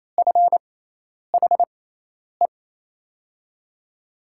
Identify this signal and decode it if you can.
Morse code, heard as a keyed tone, about 28 wpm, spaced as for 10 wpm, 700 Hz. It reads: FHE
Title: USA Call Signs - 6 Characters - 6x Character Spacing - 28wpm